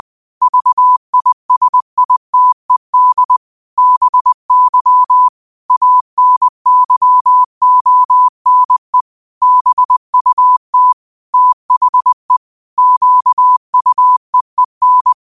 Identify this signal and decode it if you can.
Morse code, heard as a keyed tone, about 20 wpm, 1000 Hz. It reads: VISITED BY ANYONE BUT THE QUEEN